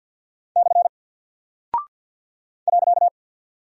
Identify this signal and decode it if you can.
Morse code, heard as a keyed tone, about 48 wpm, 700 Hz. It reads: / .
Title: Single Character (Letter:Number:Punctuation:Prosign) Alt 50wpm